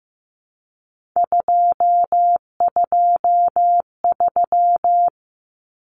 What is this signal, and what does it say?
Morse code, heard as a keyed tone, about 15 wpm, 700 Hz. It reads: 223
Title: Common Cut Numbers - 3-Digit 15wpm